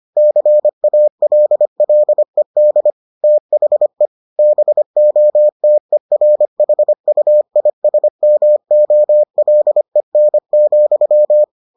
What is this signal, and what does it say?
Morse code, heard as a keyed tone, about 25 wpm, 600 Hz. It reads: CALLED THE BOTERHUISMOLEN,